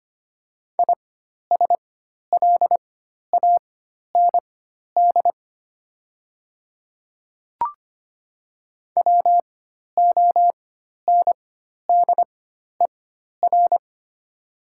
Morse code, heard as a keyed tone, about 25 wpm, spaced as for 12 wpm, 700 Hz. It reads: ISLAND WONDER
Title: 4x Character Spacing - Top 500 Words - 25wpm